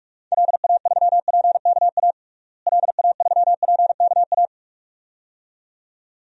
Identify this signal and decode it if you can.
Morse code, heard as a keyed tone, about 46 wpm, 700 Hz. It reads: LA3PKA LA3PKA
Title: DXCC-100 Call Signs - 6 Characters - Repeated 2x - 45wpm